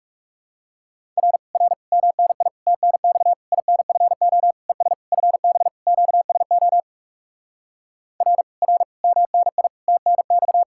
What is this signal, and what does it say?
Morse code, heard as a keyed tone, about 40 wpm, 700 Hz. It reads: R R MNI TNX INFO ES FB QSO R R MNI TNX